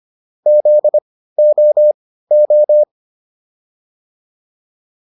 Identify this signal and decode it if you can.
Morse code, heard as a keyed tone, about 25 wpm, spaced as for 15 wpm, 600 Hz. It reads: ZOO